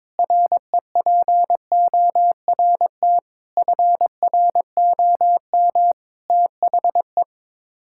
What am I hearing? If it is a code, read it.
Morse code, heard as a keyed tone, about 22 wpm, 700 Hz. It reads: REPORT FROM THE
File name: practice / 22wpm / Sets of 7 Words - 22wpm